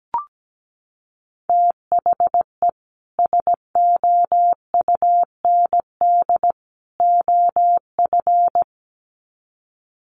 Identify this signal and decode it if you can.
Morse code, heard as a keyed tone, about 17 wpm, 700 Hz. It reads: THE SOUND OF
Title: Sets of 3 Words 17wpm